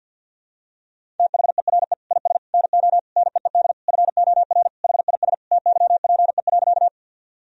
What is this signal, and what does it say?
Morse code, heard as a keyed tone, about 50 wpm, 700 Hz. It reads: THERE IS NO NEED FOR HIS TYPE.
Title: Sentences from Top 1000 Words - Review Part 3 - 50wpm